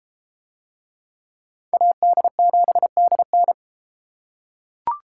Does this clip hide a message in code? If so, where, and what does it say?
Yes: Morse code from 1.7 to 3.5 s, heard as a keyed tone, about 30 wpm, 700 Hz. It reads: AD7DN